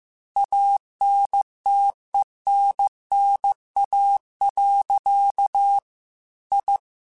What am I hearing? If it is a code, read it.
Morse code, heard as a keyed tone, about 15 wpm, 800 Hz. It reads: ANTENNA. I
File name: QSOs / QSOs-mp3s / kc7sac@13